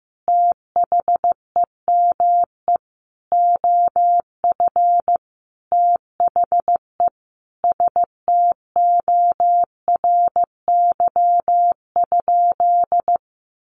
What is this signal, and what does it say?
Morse code, heard as a keyed tone, about 15 wpm, 700 Hz. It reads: THEME OF THE STORY?
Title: Sentences from Top 1100 Words 15wpm